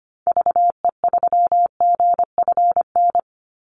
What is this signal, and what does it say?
Morse code, heard as a keyed tone, about 25 wpm, 700 Hz. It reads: VE3GFN
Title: Call Signs Repeated 2x - 25wpm